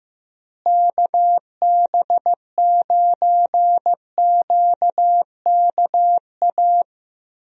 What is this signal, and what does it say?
Morse code, heard as a keyed tone, about 15 wpm, 700 Hz. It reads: KB9QKA
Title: USA Call Signs - 6 Characters - Repeated 2x - 15wpm